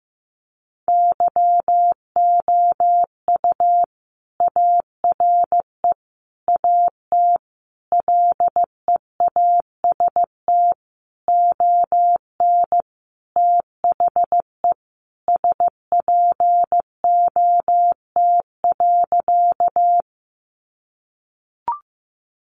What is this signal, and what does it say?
Morse code, heard as a keyed tone, about 15 wpm, 700 Hz. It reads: YOU ARE AT LEAST ON THE SPOT.
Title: Sentences from Top 1100 Words 15wpm